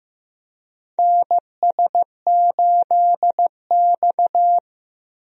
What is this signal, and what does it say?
Morse code, heard as a keyed tone, about 15 wpm, 700 Hz. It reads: NS8X
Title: USA Call Signs - 4 Characters - Repeated 2x - 15wpm